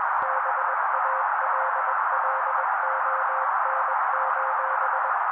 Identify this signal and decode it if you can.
Morse code, heard as a keyed tone, about 21 wpm, 530 Hz. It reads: BALLON:
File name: daily / 20wpm noise